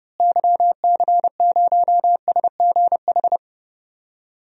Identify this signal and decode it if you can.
Morse code, heard as a keyed tone, about 30 wpm, 700 Hz. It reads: YC0SGH
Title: DXCC-100 Call Signs - 6 Characters - Repeated 3x - 30wpm